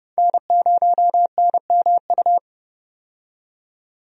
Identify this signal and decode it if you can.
Morse code, heard as a keyed tone, about 30 wpm, 700 Hz. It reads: N0NMU
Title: USA Call Signs - 5 Characters - 30wpm